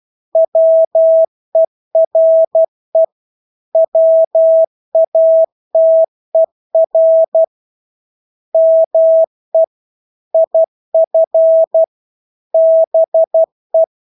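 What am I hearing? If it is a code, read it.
Morse code, heard as a keyed tone, about 12 wpm, 650 Hz. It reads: WERE WATER ME IF BE
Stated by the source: Morse practice of English words at 12 wpm.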